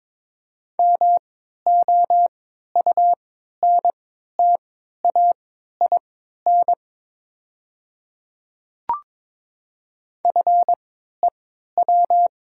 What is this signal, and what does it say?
Morse code, heard as a keyed tone, about 22 wpm, spaced as for 12 wpm, 690 Hz. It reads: MOUNTAIN FEW
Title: 3x Character Spacing - Top 300 Words - 22wpm